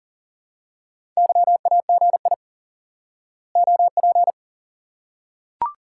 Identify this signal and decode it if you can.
Morse code, heard as a keyed tone, about 40 wpm, 700 Hz. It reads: YAGI OP
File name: IQE Course 3 - Lesson 89 - YAGI - Two Elements - 40wpm